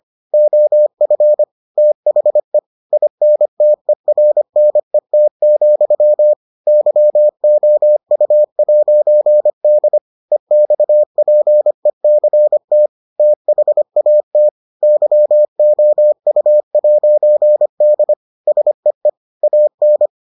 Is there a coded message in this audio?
Yes: Morse code from 0.3 to 20.1 s, heard as a keyed tone, about 25 wpm, 600 Hz. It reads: OF THE INTERNET, YOU'D EXPECT THAT YOU'D SEE AN